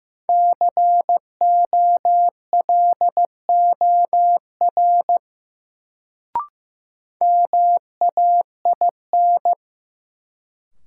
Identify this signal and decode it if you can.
Morse code, heard as a keyed tone, about 16 wpm, 700 Hz. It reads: COLOR MAIN
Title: Sentences from Top 1000 Words - Review Part 2 - New Word Drill - 15wpm